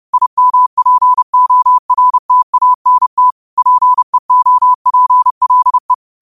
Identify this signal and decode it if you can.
Morse code, heard as a keyed tone, about 30 wpm, 1000 Hz. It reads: IMPORTANT PEOPLE
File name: qotdfiles / 30wpm